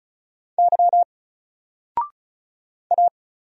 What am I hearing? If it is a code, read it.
Morse code, heard as a keyed tone, about 36 wpm, 700 Hz. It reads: Y A
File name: Single-Letter-Number-35wpm